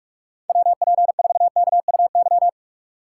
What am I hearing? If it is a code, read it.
Morse code, heard as a keyed tone, about 43 wpm, 700 Hz. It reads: WP4KUY
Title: USA Call Signs - 2x Character Spacing - 45wpm